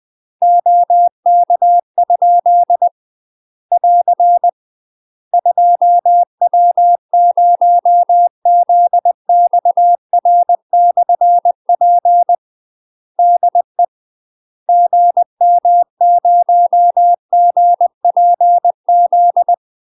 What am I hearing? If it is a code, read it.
Morse code, heard as a keyed tone, about 20 wpm, 700 Hz. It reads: OK? <AR> 2W0ZXR/P DE GM0GPZ
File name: UK MASTER QSOs TRAINING / 20WPM / UK MASTER QSOs 0056